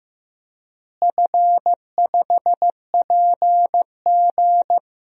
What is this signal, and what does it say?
Morse code, heard as a keyed tone, about 15 wpm, 700 Hz. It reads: F5PG
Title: DXCC-100 Call Signs - 4 Characters - Repeated 3x - 15wpm